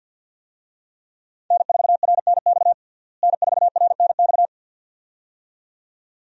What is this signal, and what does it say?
Morse code, heard as a keyed tone, about 47 wpm, 700 Hz. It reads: N4RNX N4RNX
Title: USA Call Signs - 5 Characters - Repeated 2x - 50wpm